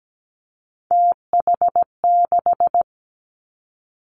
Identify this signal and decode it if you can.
Morse code, heard as a keyed tone, about 17 wpm, 700 Hz. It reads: TH6
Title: Three Challenging Letter-Number Combinations 17wpm